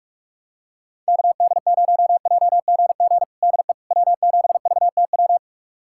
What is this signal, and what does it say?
Morse code, heard as a keyed tone, about 46 wpm, 700 Hz. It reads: KD0JGG DE W7VTW
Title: USA Callsign DE Callsign 45wpm